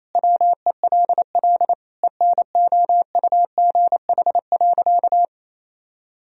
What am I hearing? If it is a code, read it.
Morse code, heard as a keyed tone, about 27 wpm, 700 Hz. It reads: WELL ENOUGH.
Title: Sentences from Top 1000 Words - Review Part 4 - 28wpm